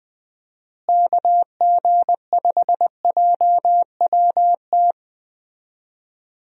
Morse code, heard as a keyed tone, about 21 wpm, 700 Hz. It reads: KG5JWT
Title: USA Call Signs - 6 Characters - Repeated 3x - 20wpm